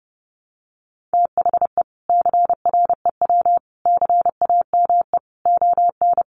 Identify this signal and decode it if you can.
Morse code, heard as a keyed tone, about 30 wpm, 700 Hz. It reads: THE CREW CAME ON
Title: Sentences from Top 1500 Words 30wpm